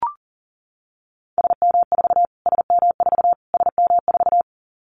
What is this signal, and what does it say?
Morse code, heard as a keyed tone, about 40 wpm, 700 Hz. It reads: SM4 SM4 SM4
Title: Callsign Prefixes for Top 25 DXCC - Repeated 3x - 40wpm